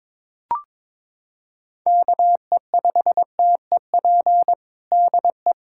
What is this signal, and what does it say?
Morse code, heard as a keyed tone, about 22 wpm, 700 Hz. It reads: KE5TEP DE
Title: USA Callsign DE Callsign - 6 Characters - 22wpm